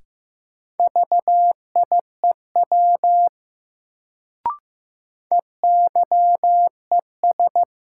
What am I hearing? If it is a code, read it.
Morse code, heard as a keyed tone, about 15 wpm, 700 Hz. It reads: VIEW EYES
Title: Sets of 5 Words - New Word Drill - 15wpm